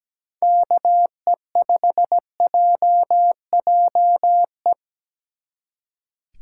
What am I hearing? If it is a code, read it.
Morse code, heard as a keyed tone, about 17 wpm, 700 Hz. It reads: KE5JJE
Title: Speed Racing - USA 2x3 Call Signs 17wpm